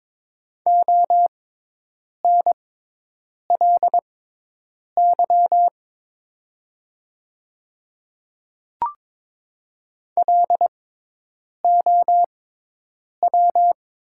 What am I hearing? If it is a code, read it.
Morse code, heard as a keyed tone, about 22 wpm, spaced as for 8 wpm, 700 Hz. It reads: ONLY LOW
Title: 6x Character Spacing - Top 200 Words - 22wpm